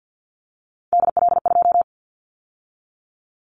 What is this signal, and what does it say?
Morse code, heard as a keyed tone, about 50 wpm, 700 Hz. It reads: DL2
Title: Callsign Prefixes for Top 50 DXCC - 50wpm